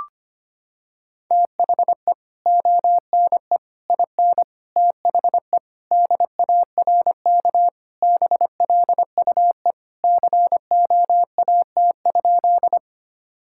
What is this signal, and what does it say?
Morse code, heard as a keyed tone, about 25 wpm, 700 Hz. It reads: THE ONE IN THE DARK BLUE COAT?
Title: Sentences from Top 1600 Words 25wpm